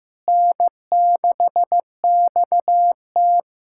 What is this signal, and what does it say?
Morse code, heard as a keyed tone, about 15 wpm, 700 Hz. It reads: N6XT
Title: USA Callsign DE Callsign - 4 Characters - 15wpm